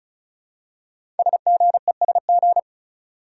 Morse code, heard as a keyed tone, about 36 wpm, 700 Hz. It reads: SGESG
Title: Code Groups - 5 Letters - 35wpm